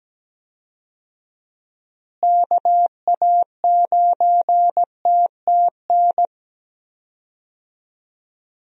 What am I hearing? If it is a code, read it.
Morse code, heard as a keyed tone, about 17 wpm, 700 Hz. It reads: KA9TTN
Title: USA Call Signs - 6 Characters - Repeated 2x - 17wpm